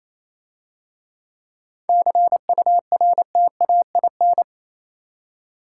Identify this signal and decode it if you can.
Morse code, heard as a keyed tone, about 28 wpm, 690 Hz. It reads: CURTAIN